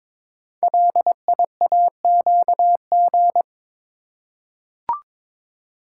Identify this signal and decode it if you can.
Morse code, heard as a keyed tone, about 22 wpm, 700 Hz. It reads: LIAQG
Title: Code Groups - 5 Letters - 22wpm